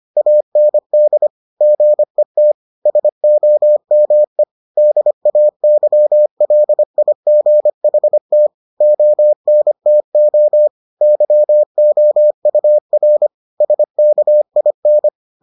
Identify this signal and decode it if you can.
Morse code, heard as a keyed tone, about 25 wpm, 600 Hz. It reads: AND GET SOME DAYLIGHT ONTO YOUR SKIN